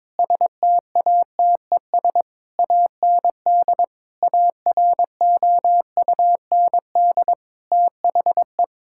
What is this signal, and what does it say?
Morse code, heard as a keyed tone, about 21 wpm, 700 Hz. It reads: STATES AND AROUND THE
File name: practice / 22wpm / Sentences with 8-Word Phrases - 22wpm